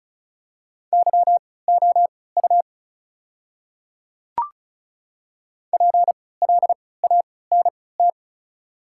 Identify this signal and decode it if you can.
Morse code, heard as a keyed tone, about 34 wpm, spaced as for 20 wpm, 700 Hz. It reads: YOU PLANT
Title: 3x Character Spacing - Top 300 Words - 35wpm